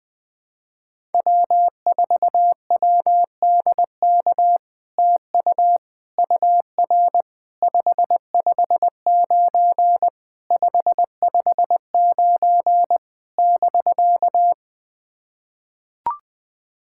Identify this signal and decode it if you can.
Morse code, heard as a keyed tone, about 20 wpm, 700 Hz. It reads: W4WDK TU UR 559 559 <BK>
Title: POTA QSO Practice - 20wpm